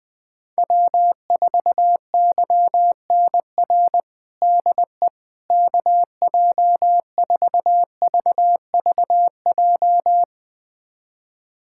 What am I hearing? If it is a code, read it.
Morse code, heard as a keyed tone, about 20 wpm, 700 Hz. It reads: W4YNR DE KJ4VVJ
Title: USA Callsign DE Callsign 20wpm